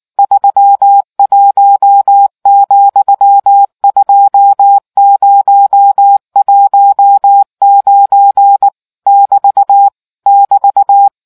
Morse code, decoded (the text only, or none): 31, 2019 <BT> <BT>